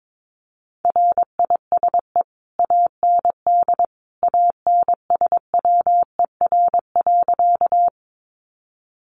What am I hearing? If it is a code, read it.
Morse code, heard as a keyed tone, about 22 wpm, 700 Hz. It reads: RISE AND ANSWER.
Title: Sentences from Top 900 Words 22wpm